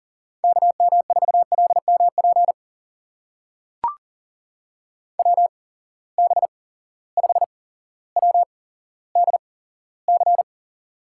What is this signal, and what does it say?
Morse code, heard as a keyed tone, about 42 wpm, spaced as for 17 wpm, 700 Hz. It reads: KM4LMP WB5WDC
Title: USA Call Signs - 8x Character Spacing - 40wpm